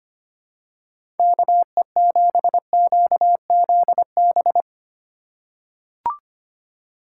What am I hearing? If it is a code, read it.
Morse code, heard as a keyed tone, about 25 wpm, 700 Hz. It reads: KE7QZB
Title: USA Call Signs - 6 Characters - 3x Character Spacing - 25wpm